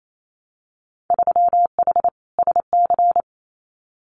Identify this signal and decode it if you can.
Morse code, heard as a keyed tone, about 28 wpm, 700 Hz. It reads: 3H SC
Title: Winter Field Day Exchanges 28wpm